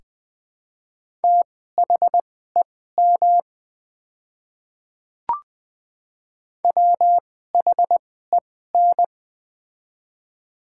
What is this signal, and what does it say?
Morse code, heard as a keyed tone, about 20 wpm, spaced as for 14 wpm, 700 Hz. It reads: THEM WHEN
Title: 2x Character Spacing - Top 100 Words - 20wpm